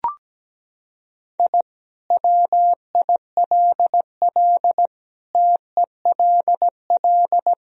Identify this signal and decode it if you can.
Morse code, heard as a keyed tone, about 17 wpm, 700 Hz. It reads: I WILL TELL
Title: Sentences from Top 1000 Words - Review Part 1 - 17wpm